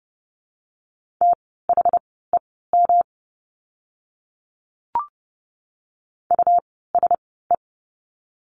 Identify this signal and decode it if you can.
Morse code, heard as a keyed tone, about 30 wpm, spaced as for 17 wpm, 700 Hz. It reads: THEM USE
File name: course / 30:10wpm / 115 Lesson 19f - Letter w - Words 30wpm